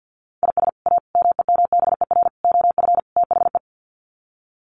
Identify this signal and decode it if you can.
Morse code, heard as a keyed tone, about 48 wpm, 700 Hz. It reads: IS A MEMBER OF THE
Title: Sets of 5 Words 50wpm